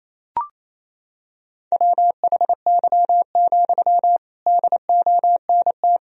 Morse code, heard as a keyed tone, about 28 wpm, 700 Hz. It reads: WHY, DONT
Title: Sentences from Top 300 Words and Abbreviations 28wpm